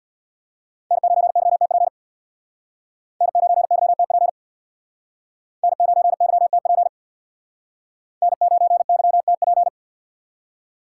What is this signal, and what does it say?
Morse code, heard as a keyed tone, about 62 wpm, 700 Hz. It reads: N9YTP N9YTP N9YTP N9YTP
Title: Speed Racing - USA 1x3 Call Signs 50wpm